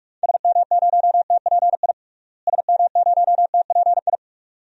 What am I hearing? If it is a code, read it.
Morse code, heard as a keyed tone, about 46 wpm, 700 Hz. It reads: SM0TPI SM0TPI
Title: DXCC-100 Call Signs - 6 Characters - Repeated 2x - 45wpm